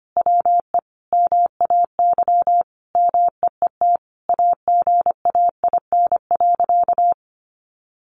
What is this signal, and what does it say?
Morse code, heard as a keyed tone, about 25 wpm, 700 Hz. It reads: WE MAY MEET AGAIN.
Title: Sentences from Top 1100 Words 25wpm